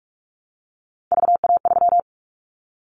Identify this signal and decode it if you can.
Morse code, heard as a keyed tone, about 46 wpm, 700 Hz. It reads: VA3
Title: Callsign Prefixes for Top 50 DXCC - 45wpm